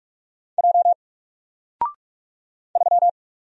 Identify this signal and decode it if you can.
Morse code, heard as a keyed tone, about 43 wpm, 700 Hz. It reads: J 3
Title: Single Character (Letter:Number:Punctuation:Prosign) Alt 45wpm